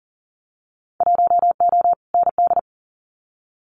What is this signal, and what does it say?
Morse code, heard as a keyed tone, about 40 wpm, 700 Hz. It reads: 1O ND
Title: Winter Field Day Exchanges 40wpm